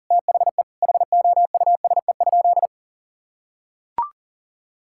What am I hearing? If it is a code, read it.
Morse code, heard as a keyed tone, about 40 wpm, 700 Hz. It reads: THE HOUSE?
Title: Sentences from Top 1000 Words - Review Part 5 - 40wpm